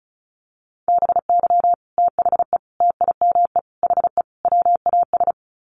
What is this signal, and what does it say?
Morse code, heard as a keyed tone, about 34 wpm, 700 Hz. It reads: BY THE TIME HE WAS
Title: Sets of 5 Words Encore 35wpm